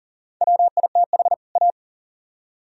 Morse code, heard as a keyed tone, about 53 wpm, spaced as for 47 wpm, 700 Hz. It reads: WITH A
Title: Speed-Racing - Sets of 2 Words 40wpm